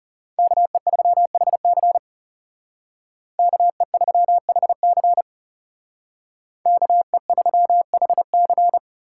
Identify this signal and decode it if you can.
Morse code, heard as a keyed tone, about 39 wpm, 700 Hz. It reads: KE3HC KE3HC KE3HC